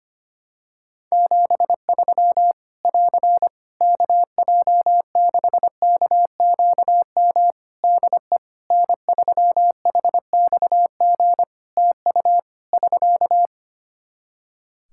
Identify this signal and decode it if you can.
Morse code, heard as a keyed tone, about 25 wpm, 700 Hz. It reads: 73 <AR> KJ6KQM DE N3HXG TU <SK>
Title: Practice QSO - Exchange 4 - Ending - 25wpm